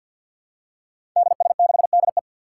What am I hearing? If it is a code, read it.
Morse code, heard as a keyed tone, about 52 wpm, 700 Hz. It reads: DS6DE
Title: Five Challenging Letter-Number Combinations 50wpm